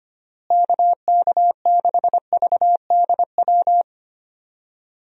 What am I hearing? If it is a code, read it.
Morse code, heard as a keyed tone, about 25 wpm, 700 Hz. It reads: KK6VDW
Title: USA Call Signs - 6 Characters - Repeated 3x - 25wpm